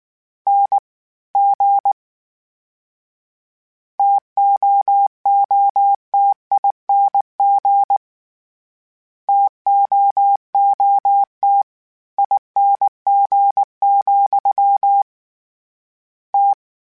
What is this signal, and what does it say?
Morse code, heard as a keyed tone, about 19 wpm, 800 Hz. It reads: N G TOOTING TOOT ING, T
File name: Variable Spacing / Variable Spacing-mp3s / TOs @18